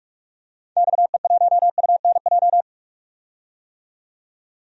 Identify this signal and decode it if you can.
Morse code, heard as a keyed tone, about 46 wpm, 700 Hz. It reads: XE1UNJ
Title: DXCC-100 Call Signs - 6 Characters - 45wpm